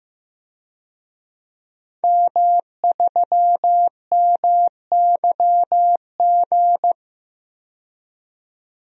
Morse code, heard as a keyed tone, about 15 wpm, 700 Hz. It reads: M3MYG
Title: DXCC-100 Call Signs - 5 Characters - 15wpm